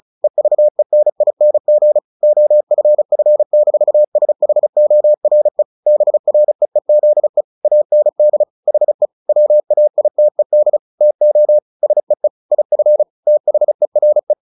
Morse code, heard as a keyed tone, about 34 wpm, 600 Hz. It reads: EVENING OFF-SHORE BREEZE AND HE WAITED TO SEE IF THERE
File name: the old man and the sea 2 35wpm